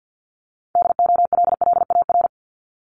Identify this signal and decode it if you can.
Morse code, heard as a keyed tone, about 50 wpm, 700 Hz. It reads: DOLLAR
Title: Multi-letter Phonemes Repeated 3x 50wpm